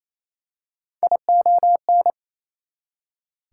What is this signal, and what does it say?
Morse code, heard as a keyed tone, about 28 wpm, 700 Hz. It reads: ION
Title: Prefixes and Suffixes 28wpm